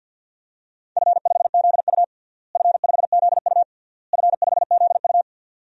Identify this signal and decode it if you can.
Morse code, heard as a keyed tone, about 52 wpm, 700 Hz. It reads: F5ZU F5ZU F5ZU
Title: DXCC-100 Call Signs - 4 Characters - Repeated 3x - 50wpm